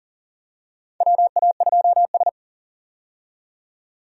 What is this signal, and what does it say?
Morse code, heard as a keyed tone, about 40 wpm, 700 Hz. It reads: WA2S
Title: USA Call Signs - 4 Characters - 40wpm